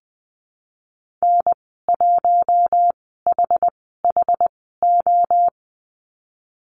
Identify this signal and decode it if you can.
Morse code, heard as a keyed tone, about 20 wpm, spaced as for 15 wpm, 700 Hz. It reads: N1HHO
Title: 062 Lesson 11g - Letter h - Call Signs 20wpm